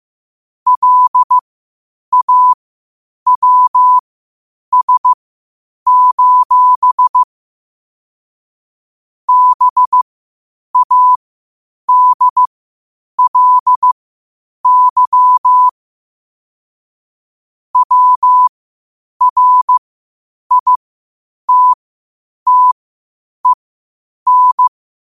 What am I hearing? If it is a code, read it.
Morse code, heard as a keyed tone, about 15 wpm, spaced as for 8 wpm, 1000 Hz. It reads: LAWS: BADLY WRITTEN
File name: qotdfiles / 8wpm